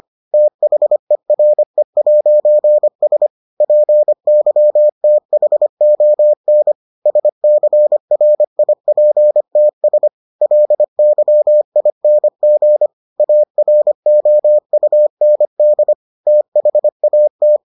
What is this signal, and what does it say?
Morse code, heard as a keyed tone, about 25 wpm, 600 Hz. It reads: THERE'S PYTHON SCRIPTS LYING AROUND THAT